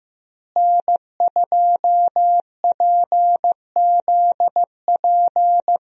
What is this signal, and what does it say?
Morse code, heard as a keyed tone, about 15 wpm, 700 Hz. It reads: N2PZP
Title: USA Call Signs - 5 Characters - Repeated 2x - 15wpm